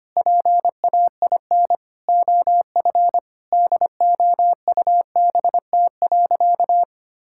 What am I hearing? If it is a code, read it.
Morse code, heard as a keyed tone, about 25 wpm, 700 Hz. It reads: PAIN OF DOUBT.
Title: Sentences from Top 1600 Words 25wpm